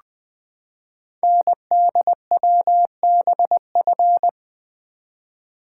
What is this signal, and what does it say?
Morse code, heard as a keyed tone, about 20 wpm, 700 Hz. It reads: NDWBF
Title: Code Groups - 5 Letters - 20wpm